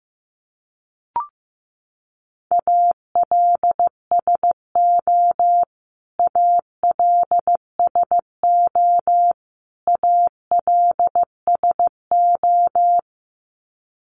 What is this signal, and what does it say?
Morse code, heard as a keyed tone, about 14 wpm, 700 Hz. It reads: ALSO ALSO ALSO
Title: Top 200 Words Repeated 15wpm